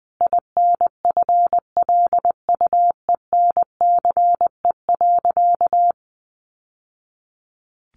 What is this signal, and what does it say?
Morse code, heard as a keyed tone, about 20 wpm, 700 Hz. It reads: INFLUENCE.
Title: Sentences from Top 1100 Words 20wpm